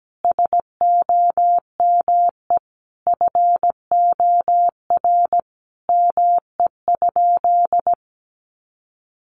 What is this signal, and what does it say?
Morse code, heard as a keyed tone, about 17 wpm, 700 Hz. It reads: SOME FOR ME?
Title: Sentences from Top 200 Words 17wpm